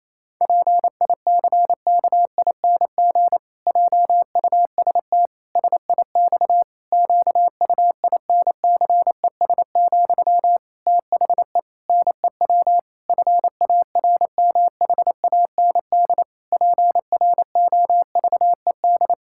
Morse code, heard as a keyed tone, about 28 wpm, 700 Hz. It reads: PICKING JUST SIX QUINCES, THE NEW FARMHAND PROVED
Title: Pangram Challenge v2 - 28wpm